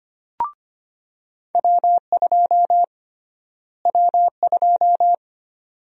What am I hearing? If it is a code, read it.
Morse code, heard as a keyed tone, about 25 wpm, 700 Hz. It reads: W2 W2
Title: USA Call Sign Prefixes - Repeated 2x - 25wpm